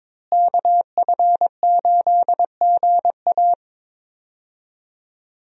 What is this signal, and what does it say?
Morse code, heard as a keyed tone, about 22 wpm, 700 Hz. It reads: KF8GA
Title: USA Call Signs - 5 Characters - Repeated 2x - 22wpm